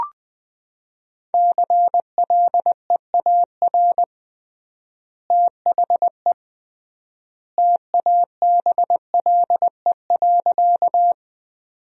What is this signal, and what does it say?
Morse code, heard as a keyed tone, about 20 wpm, 700 Hz. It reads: CLEAR THE TABLE.
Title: 3x Word-Spacing - Sentences from Top 500 Words - 20wpm